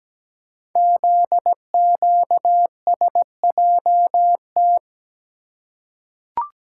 Morse code, heard as a keyed tone, about 17 wpm, 700 Hz. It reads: ZQSJT
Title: Code Groups - 5 Letters - 17wpm